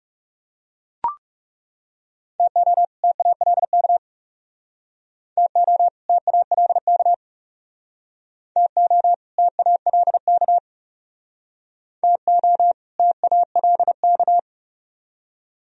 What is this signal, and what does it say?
Morse code, heard as a keyed tone, about 40 wpm, 700 Hz. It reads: TO TALK TO TALK TO TALK TO TALK